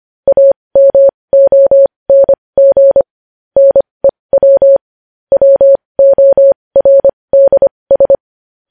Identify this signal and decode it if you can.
Morse code, heard as a keyed tone, about 25 wpm, spaced as for 20 wpm, 550 Hz. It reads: AMONG NEW WORDS
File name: podcast / News headlines-1x-25x20